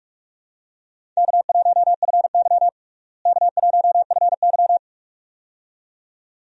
Morse code, heard as a keyed tone, about 46 wpm, 700 Hz. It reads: K1FY K1FY
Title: USA Call Signs - 4 Characters - Repeated 2x - 45wpm